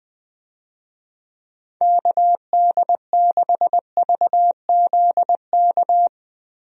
Morse code, heard as a keyed tone, about 20 wpm, 700 Hz. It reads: KD6VZK
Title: USA Call Signs - 6 Characters - Repeated 3x - 20wpm